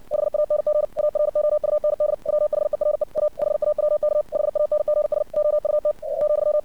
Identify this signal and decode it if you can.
Morse code, heard as a keyed tone, about 60 wpm, 600 Hz. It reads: HANG AROUND WHERE A HAWK HANGS OUT ,
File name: archived / CRSnet